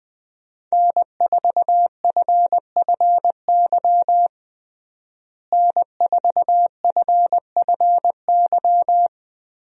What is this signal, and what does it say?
Morse code, heard as a keyed tone, about 20 wpm, 700 Hz. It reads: N4FFY N4FFY
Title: USA Call Signs - 5 Characters - Repeated 3x - 20wpm